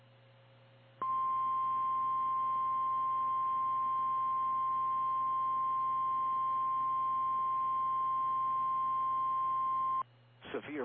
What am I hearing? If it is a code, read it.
Emergency Alert System (SAME) tones (the 1050 Hz attention tone), with speech. It is heard from 1.0 to 10.1 s.
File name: eas / KKPRKXPN / KKPRKXPN